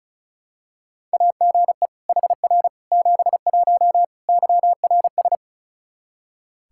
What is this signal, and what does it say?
Morse code, heard as a keyed tone, about 36 wpm, 700 Hz. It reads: AGE HR 71 YRS
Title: Practice QSO - Exchange 3 - More Info - 35wpm